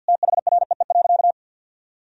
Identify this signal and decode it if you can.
Morse code, heard as a keyed tone, about 50 wpm, 700 Hz. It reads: THREE.
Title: Sentences from Top 1000 Words - Review Part 4 - 50wpm